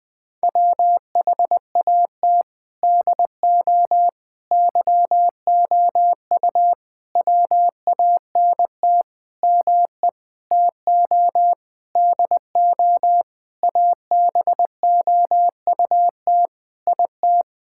Morse code, heard as a keyed tone, about 21 wpm, 700 Hz. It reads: WHAT DO YOU WANT ME TO DO ABOUT IT
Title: Sentences with 7-Word Phrases - Spoken Beginning - 20wpm